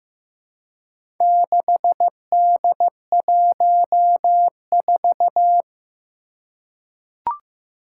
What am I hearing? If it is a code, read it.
Morse code, heard as a keyed tone, about 16 wpm, 700 Hz. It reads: 6D14
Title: Four Challenging Letter-Number Combinations 15wpm